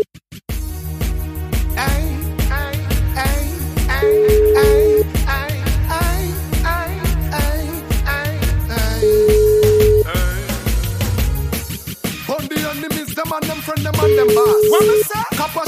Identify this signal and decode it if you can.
Ringback tone (Europe, 425 Hz), with noise.